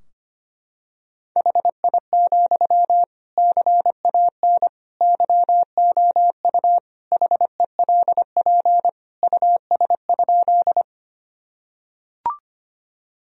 Morse code, heard as a keyed tone, about 25 wpm, 700 Hz. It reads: HI, CAN YOU HELP US?
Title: Sentences from Top 1000 Words - Review Part 3 - 25wpm